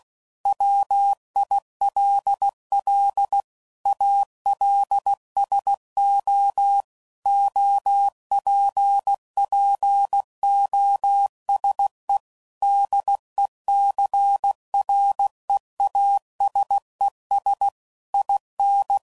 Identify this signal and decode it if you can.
Morse code, heard as a keyed tone, about 16 wpm, 800 Hz. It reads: WILL ALSO OPPOSE DECREASES IN
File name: CW Code Groups @15WPM / Technical Subjects / XC